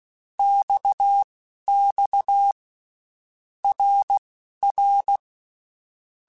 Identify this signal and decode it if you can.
Morse code, heard as a keyed tone, about 16 wpm, spaced as for 12 wpm, 800 Hz. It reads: XX RR
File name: Basic Letter Groups By7Groups / RKPX / RKPX@10